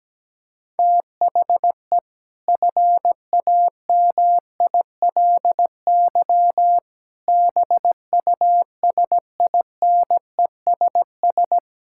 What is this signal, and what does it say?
Morse code, heard as a keyed tone, about 17 wpm, 700 Hz. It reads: THE FAMILY BUSINESS
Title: Sentences from Top 1000 Words - Review Part 6 - 17wpm